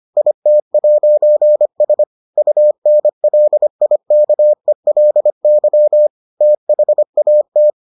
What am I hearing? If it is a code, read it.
Morse code, heard as a keyed tone, about 25 wpm, 600 Hz. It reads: IT'S UNLIKELY THAT